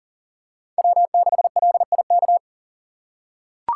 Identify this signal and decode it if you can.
Morse code, heard as a keyed tone, about 37 wpm, 700 Hz. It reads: W6LIK
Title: USA Call Signs - 5 Characters - 2x Character Spacing - 40wpm